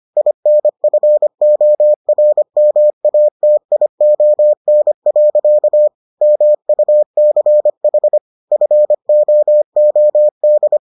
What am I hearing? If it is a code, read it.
Morse code, heard as a keyed tone, about 25 wpm, 600 Hz. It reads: INFORMATION. MUCH FOOD